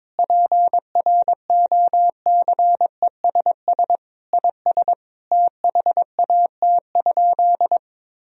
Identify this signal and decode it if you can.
Morse code, heard as a keyed tone, about 22 wpm, 700 Hz. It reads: PROCESS IS THAT?
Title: Sentences from Top 1000 Words - Review Part 4 - 22wpm